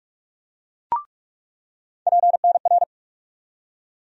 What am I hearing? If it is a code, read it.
Morse code, heard as a keyed tone, about 46 wpm, 700 Hz. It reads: PNR
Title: USA Call Sign Postfixes - 3 Letters 45wpm